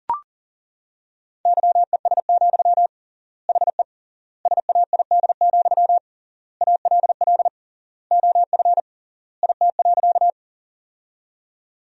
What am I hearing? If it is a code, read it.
Morse code, heard as a keyed tone, about 37 wpm, 700 Hz. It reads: YES, HE SAID, ALL OF IT.
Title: 3x Word-Spacing - Sentences from Top 500 Words - 40wpm